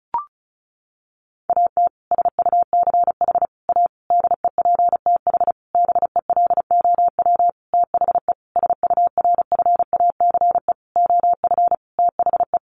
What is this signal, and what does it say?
Morse code, heard as a keyed tone, about 34 wpm, 700 Hz. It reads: AT SUCH A DEPTH BELOW THE SURFACE OF THE
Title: Sentences from Top 1200 Words 35wpm